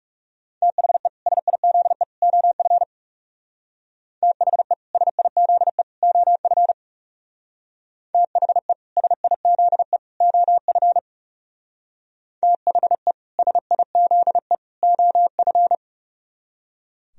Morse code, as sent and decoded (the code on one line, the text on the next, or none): - .... / ... .. --.. / --- ..-. / - .... / ... .. --.. / --- ..-. / - .... / ... .. --.. / --- ..-. / - .... / ... .. --.. / --- ..-.
THE SIZE OF THE SIZE OF THE SIZE OF THE SIZE OF